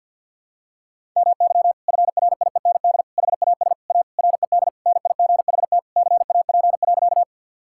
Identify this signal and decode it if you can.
Morse code, heard as a keyed tone, about 50 wpm, 700 Hz. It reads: MY FRIEND HAS A RED NIGHT CAP.
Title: Sentences from Top 1600 Words 50wpm